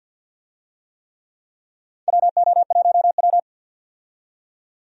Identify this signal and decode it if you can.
Morse code, heard as a keyed tone, about 52 wpm, 700 Hz. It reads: WO1W
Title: USA Call Signs - 4 Characters - 50wpm